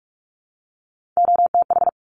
Morse code, heard as a keyed tone, about 46 wpm, 700 Hz. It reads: OTH
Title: Top Three Letter Combinations in Words 45wpm